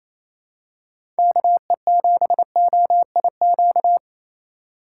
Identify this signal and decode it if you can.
Morse code, heard as a keyed tone, about 28 wpm, 700 Hz. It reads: KE7OIQ
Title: USA Call Signs - 6 Characters - 2x Character Spacing - 28wpm